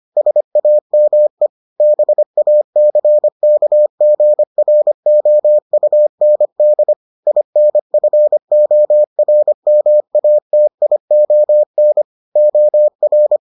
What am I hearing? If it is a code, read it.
Morse code, heard as a keyed tone, about 25 wpm, 600 Hz. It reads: SAME BACKGROUND INFORMATION OR